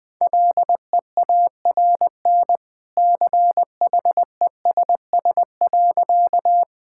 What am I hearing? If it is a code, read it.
Morse code, heard as a keyed tone, about 20 wpm, 700 Hz. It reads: LEARN CHESS.
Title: Easy Sentences with Hints - 20wpm